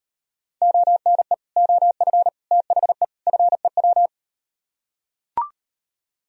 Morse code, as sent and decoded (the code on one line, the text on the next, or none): --- -. / --- ..-. / - .... / ..-. .--
ONE OF THE FEW